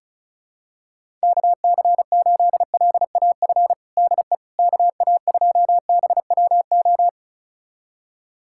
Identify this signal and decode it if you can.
Morse code, heard as a keyed tone, about 36 wpm, 700 Hz. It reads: KC8LAF DE KA2BWO
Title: USA Callsign DE Callsign - 6 Characters - 35wpm